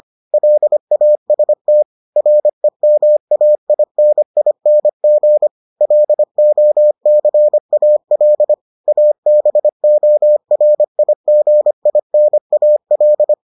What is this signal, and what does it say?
Morse code, heard as a keyed tone, about 25 wpm, 600 Hz. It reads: LAST REMAINING LOCAL ABORIGINAL